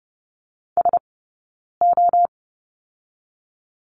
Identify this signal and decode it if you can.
Morse code, heard as a keyed tone, about 30 wpm, spaced as for 9 wpm, 700 Hz. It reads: SO
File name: Lesson 20d - Number 3 - Two Characters 30wpm